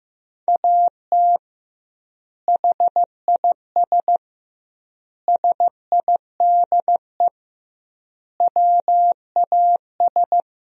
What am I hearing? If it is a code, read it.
Morse code, heard as a keyed tone, about 15 wpm, 700 Hz. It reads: AT HIS SIDE WAS
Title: 2x Word-Spacing - Sentences from Top 500 Words - 15wpm